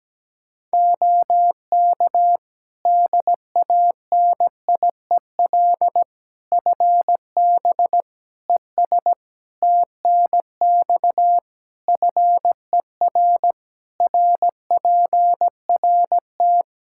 Morse code, heard as a keyed tone, about 17 wpm, 700 Hz. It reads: OK DANIEL FB ES TNX FER RPRT
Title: K1USN SST QSO Practice 17wpm